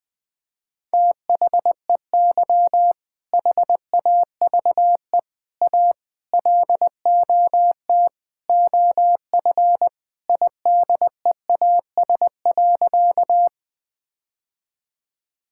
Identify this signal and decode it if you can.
Morse code, heard as a keyed tone, about 21 wpm, 700 Hz. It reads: THEY HAVE A LOT OF IDEAS.
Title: Sentences with 4-Word Phrases - 20wpm